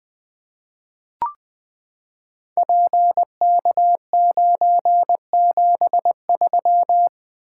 Morse code, heard as a keyed tone, about 20 wpm, 700 Hz. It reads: PK973
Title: Code Groups - 5 Letter:Numbers - 20wpm